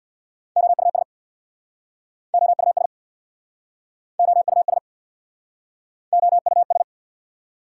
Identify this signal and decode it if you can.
Morse code, heard as a keyed tone, about 69 wpm, 690 Hz. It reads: OUS OUS OUS OUS